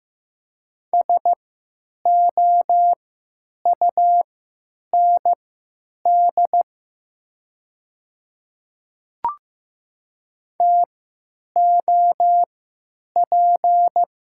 Morse code, heard as a keyed tone, about 15 wpm, spaced as for 9 wpm, 700 Hz. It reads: SOUND TOP
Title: 3x Character Spacing - Top 500 Words - 15wpm